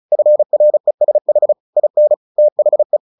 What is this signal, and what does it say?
Morse code, heard as a keyed tone, about 34 wpm, 600 Hz. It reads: FRESH IN THE
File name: the old man and the sea / the old man and the sea 1 35wpm